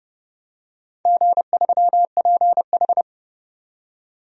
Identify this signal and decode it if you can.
Morse code, heard as a keyed tone, about 30 wpm, 700 Hz. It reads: G3PH
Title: DXCC-100 Call Signs - 4 Characters - 30wpm